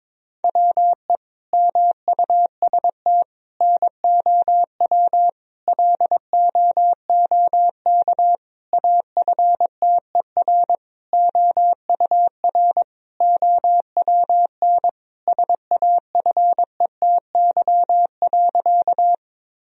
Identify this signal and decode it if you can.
Morse code, heard as a keyed tone, about 22 wpm, 700 Hz. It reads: WE MUST NOW LOOK AFTER OUR OWN SAFETY.
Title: Sentences from Top 1000 Words - Review Part 4 - 22wpm